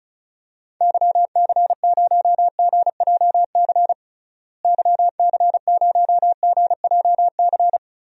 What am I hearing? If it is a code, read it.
Morse code, heard as a keyed tone, about 36 wpm, 700 Hz. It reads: YC0GJC YC0GJC
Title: DXCC-100 Call Signs - 6 Characters - Repeated 3x - 35wpm